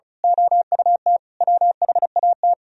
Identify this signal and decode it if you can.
Morse code, heard as a keyed tone, about 33 wpm, 700 Hz. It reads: OUT WHAT
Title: Speed-Racing - Sets of 5 Words 35wpm